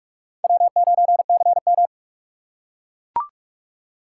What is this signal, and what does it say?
Morse code, heard as a keyed tone, about 44 wpm, 700 Hz. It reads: W9CM